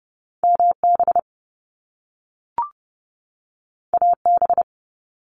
Morse code, heard as a keyed tone, about 30 wpm, 700 Hz. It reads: MB AB
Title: Canadian Provinces and Territories 30wpm